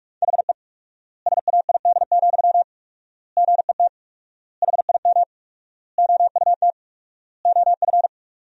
Morse code, heard as a keyed tone, about 43 wpm, 700 Hz. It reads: HE SAID, GET HIM OUT OF
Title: 4x Word-Spacing - Sentences from Top 400 Words - 45wpm